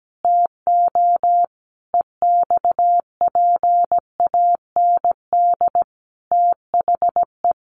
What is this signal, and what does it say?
Morse code, heard as a keyed tone, about 17 wpm, 700 Hz. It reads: TO EXPAND THE
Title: Sentences from Top 1400 Words 17wpm